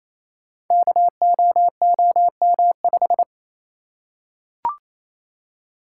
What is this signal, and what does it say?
Morse code, heard as a keyed tone, about 28 wpm, 700 Hz. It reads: KOOM5
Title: Code Groups - 5 Letter:Numbers - 28wpm